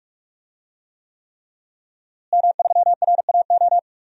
Morse code, heard as a keyed tone, about 48 wpm, 700 Hz. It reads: M3RAO
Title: DXCC-100 Call Signs - 5 Characters - Repeated 3x - 45wpm